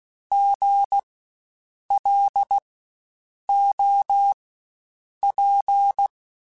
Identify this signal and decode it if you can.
Morse code, heard as a keyed tone, about 16 wpm, spaced as for 7 wpm, 800 Hz. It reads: GLOP
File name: Spaced Alphas / Spaced Alphas mp3s / Gs sp @15